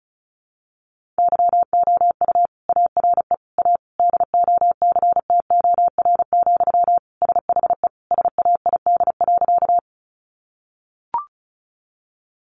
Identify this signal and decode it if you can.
Morse code, heard as a keyed tone, about 34 wpm, 700 Hz. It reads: YOU ARE A DOCTOR, SHE SAID.
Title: Sentences from Top 1500 Words 35wpm